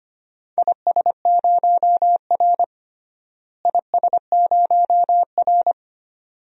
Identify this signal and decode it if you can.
Morse code, heard as a keyed tone, about 25 wpm, 700 Hz. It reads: IS0R IS0R
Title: DXCC-100 Call Signs - 4 Characters - Repeated 2x - 25wpm